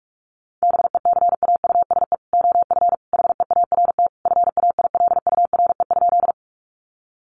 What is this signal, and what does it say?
Morse code, heard as a keyed tone, about 44 wpm, 700 Hz. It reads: BECAUSE OF HEART FAILURE?
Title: Sentences from Top 1200 Words 45wpm